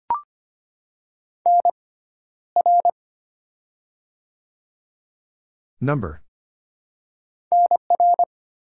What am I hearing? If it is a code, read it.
Morse code, heard as a keyed tone, about 25 wpm, spaced as for 13 wpm, 700 Hz. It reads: NR NR
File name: Top 90 QSO Elements - 6x Character Spacing - 25wpm